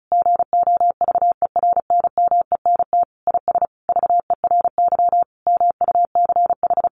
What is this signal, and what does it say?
Morse code, heard as a keyed tone, about 34 wpm, 700 Hz. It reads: GOVERNMENT IS VERY MUCH
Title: Sentences from Top 1300 Words 35wpm